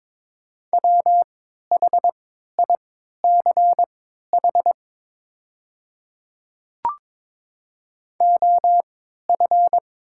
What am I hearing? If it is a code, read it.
Morse code, heard as a keyed tone, about 22 wpm, spaced as for 12 wpm, 700 Hz. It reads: WHICH OF
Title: 3x Character Spacing - Top 100 Words - 22wpm